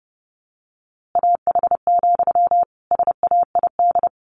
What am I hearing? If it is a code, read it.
Morse code, heard as a keyed tone, about 30 wpm, 700 Hz. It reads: AH, SAID
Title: A Sound Of Thunder 30wpm